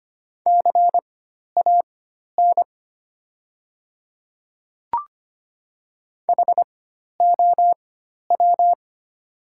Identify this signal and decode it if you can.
Morse code, heard as a keyed tone, about 25 wpm, spaced as for 12 wpm, 700 Hz. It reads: CAN HOW
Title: 4x Character Spacing - Top 100 Words - 25wpm